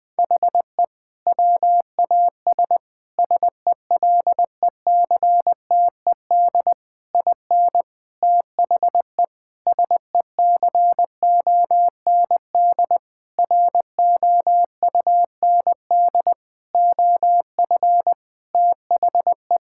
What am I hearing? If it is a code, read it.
Morse code, heard as a keyed tone, about 20 wpm, 700 Hz. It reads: HE WAS SELECTED IN THE SECOND ROUND OF THE
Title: Sentences with 6-Word Phrases - 20wpm